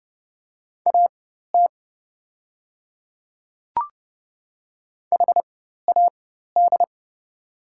Morse code, heard as a keyed tone, about 30 wpm, spaced as for 14 wpm, 700 Hz. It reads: AT HAD
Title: 4x Character Spacing - Top 100 Words - 30wpm